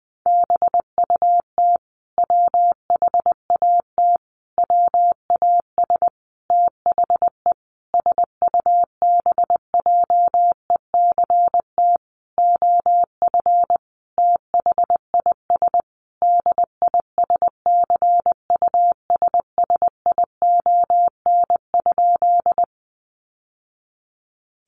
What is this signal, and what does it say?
Morse code, heard as a keyed tone, about 20 wpm, 700 Hz. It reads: BUT WHAT WAS THE SUBJECT OF THIS DISCUSSION?
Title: Sentences from Top 1100 Words 20wpm